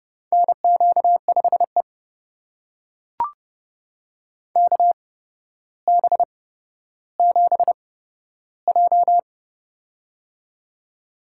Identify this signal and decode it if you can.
Morse code, heard as a keyed tone, about 31 wpm, spaced as for 13 wpm, 700 Hz. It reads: NQ5E KB7J
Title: USA Call Signs - 4 Characters - 8x Character Spacing - 30wpm